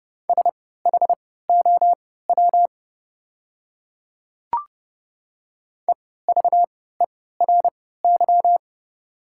Morse code, heard as a keyed tone, about 29 wpm, spaced as for 17 wpm, 700 Hz. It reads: SHOW EVERY
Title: 3x Character Spacing - Top 300 Words - 30wpm